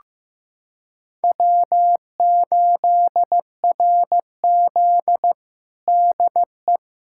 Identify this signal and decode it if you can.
Morse code, heard as a keyed tone, about 15 wpm, 700 Hz. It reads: W8RZ DE
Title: USA Callsign DE Callsign - 4 Characters - 15wpm